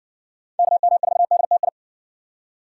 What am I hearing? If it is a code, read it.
Morse code, heard as a keyed tone, about 57 wpm, 700 Hz. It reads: KM4DTI